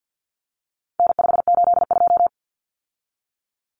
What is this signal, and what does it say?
Morse code, heard as a keyed tone, about 50 wpm, 700 Hz. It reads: N582
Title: Common Cut Numbers - 4-Digit 50wpm